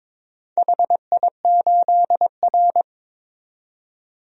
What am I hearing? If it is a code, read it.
Morse code, heard as a keyed tone, about 23 wpm, 700 Hz. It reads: HI8R
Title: DXCC-100 Call Signs - 4 Characters - 22wpm